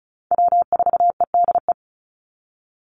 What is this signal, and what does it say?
Morse code, heard as a keyed tone, about 34 wpm, 700 Hz. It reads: W4EDE
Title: Call Signs Repeated 2x - 35wpm